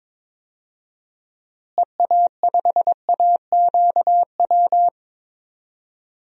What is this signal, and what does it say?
Morse code, heard as a keyed tone, about 21 wpm, 700 Hz. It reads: EA5AQW
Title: DXCC-100 Call Signs - 6 Characters - Repeated 3x - 22wpm